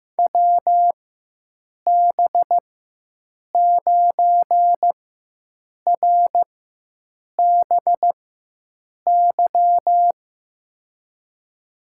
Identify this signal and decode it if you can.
Morse code, heard as a keyed tone, about 15 wpm, spaced as for 7 wpm, 700 Hz. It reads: WB9RBY
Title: USA Call Signs - 6 Characters - 4x Character Spacing - 15wpm